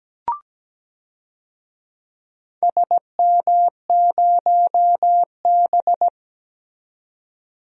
Morse code, heard as a keyed tone, about 17 wpm, 700 Hz. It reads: SM0B
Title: DXCC-100 Call Signs - 4 Characters - 17wpm